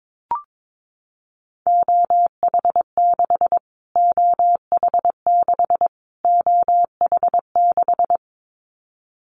Morse code, heard as a keyed tone, about 22 wpm, 700 Hz. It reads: OH6 OH6 OH6
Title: Callsign Prefixes for Top 25 DXCC - Repeated 3x - 22wpm